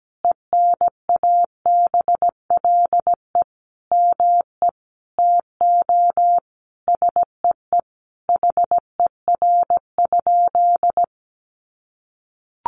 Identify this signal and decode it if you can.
Morse code, heard as a keyed tone, about 17 wpm, 700 Hz. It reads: ENABLE ME TO SEE HER?
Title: Sentences from Top 1500 Words 17wpm